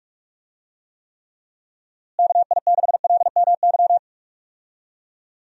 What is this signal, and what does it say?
Morse code, heard as a keyed tone, about 46 wpm, 700 Hz. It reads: KI6LMY
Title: USA Call Signs - 6 Characters - Repeated 3x - 45wpm